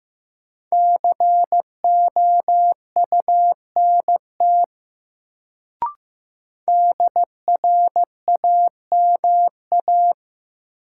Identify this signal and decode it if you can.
Morse code, heard as a keyed tone, about 15 wpm, 700 Hz. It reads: COUNT DRAMA
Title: Sentences from Top 1500 Words - New Word Drill - 15wpm